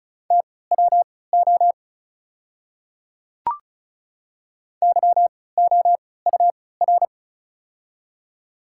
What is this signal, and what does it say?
Morse code, heard as a keyed tone, about 34 wpm, spaced as for 20 wpm, 700 Hz. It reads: TWO YOUR
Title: 3x Character Spacing - Top 200 Words - 35wpm